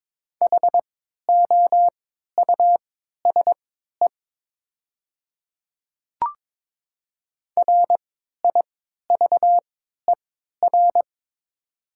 Morse code, heard as a keyed tone, about 22 wpm, spaced as for 12 wpm, 700 Hz. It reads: HOUSE RIVER
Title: 3x Character Spacing - Top 300 Words - 22wpm